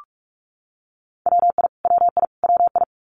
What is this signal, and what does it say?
Morse code, heard as a keyed tone, about 44 wpm, 700 Hz. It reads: WI WI WI
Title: US States Repeated 3x 45wpm